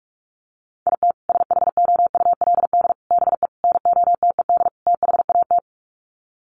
Morse code, heard as a keyed tone, about 44 wpm, 700 Hz. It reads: IT SHOULD BE NOTED THAT
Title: Sets of 5 Words 45wpm